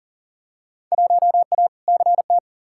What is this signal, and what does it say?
Morse code, heard as a keyed tone, about 40 wpm, 700 Hz. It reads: 1A CT
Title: Field Day Exchanges Encore - 40wpm